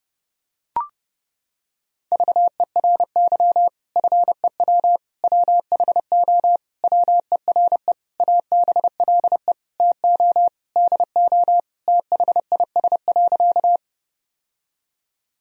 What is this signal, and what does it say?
Morse code, heard as a keyed tone, about 30 wpm, 700 Hz. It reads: VERY FEW WHO WERE ABLE TO DO THIS.
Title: Sentences from Top 1000 Words - Review Part 2 - 30wpm